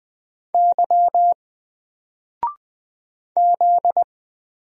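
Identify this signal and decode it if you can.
Morse code, heard as a keyed tone, about 20 wpm, 700 Hz. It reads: Y Z
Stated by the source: Single letter and number Morse drill at 20 wpm, from N4LCD.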